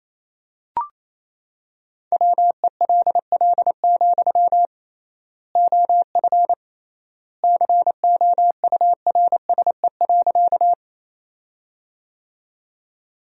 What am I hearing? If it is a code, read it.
Morse code, heard as a keyed tone, about 28 wpm, 700 Hz. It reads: WELL, OF COURSE.
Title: 3x Word-Spacing - Sentences from Top 500 Words - 28wpm